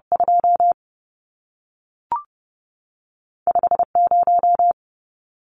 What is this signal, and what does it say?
Morse code, heard as a keyed tone, about 30 wpm, 700 Hz. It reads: 2 5 0
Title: Numbers - 2-Digit 30wpm